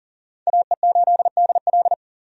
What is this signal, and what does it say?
Morse code, heard as a keyed tone, about 37 wpm, 700 Hz. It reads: AE8DL
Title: USA Call Signs - 5 Characters - 40wpm